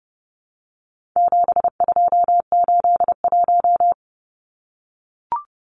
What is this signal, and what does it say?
Morse code, heard as a keyed tone, about 30 wpm, 700 Hz. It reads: 7281
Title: Numbers - 4-Digit 30wpm